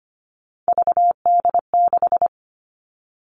Morse code, heard as a keyed tone, about 25 wpm, 700 Hz. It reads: VD6
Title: All Cut Numbers - 3-Digit 25wpm